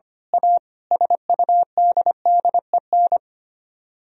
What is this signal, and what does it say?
Morse code, heard as a keyed tone, about 24 wpm, 700 Hz. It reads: A SUDDEN
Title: Speed-Racing - Sets of 5 Words 25wpm